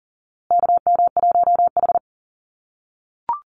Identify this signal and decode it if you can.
Morse code, heard as a keyed tone, about 40 wpm, 700 Hz. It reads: KM1H
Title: USA Call Signs Repeated 3x 40wpm